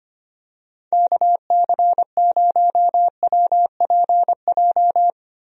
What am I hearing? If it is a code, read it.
Morse code, heard as a keyed tone, about 25 wpm, 700 Hz. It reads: KC0WPJ
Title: USA Call Signs - 6 Characters - Repeated 2x - 25wpm